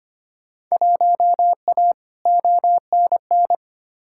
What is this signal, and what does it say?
Morse code, heard as a keyed tone, about 25 wpm, 700 Hz. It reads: 1A ONN
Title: Field Day Exchanges Encore - 25wpm